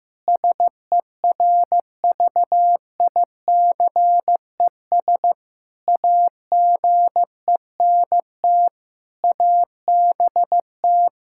Morse code, heard as a keyed tone, about 16 wpm, 700 Hz. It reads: SERVICES AGENT ABT
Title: K1USN SST QSO Practice 15wpm